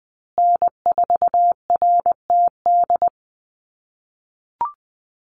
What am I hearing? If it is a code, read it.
Morse code, heard as a keyed tone, about 20 wpm, 700 Hz. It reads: N4RTD
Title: USA Call Signs Repeated 3x 20wpm